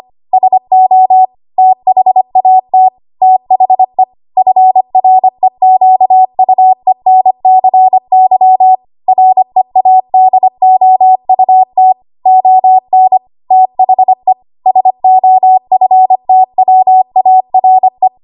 Morse code, heard as a keyed tone, about 25 wpm, 750 Hz. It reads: SO THAT THE FREQUENCY READOUT ON THE SOFTWARE
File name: Archive / 25 WPM / 25WPM